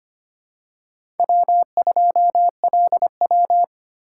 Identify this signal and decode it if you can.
Morse code, heard as a keyed tone, about 25 wpm, 700 Hz. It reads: W2LW
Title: USA Call Signs - 4 Characters - 4x Character Spacing - 25wpm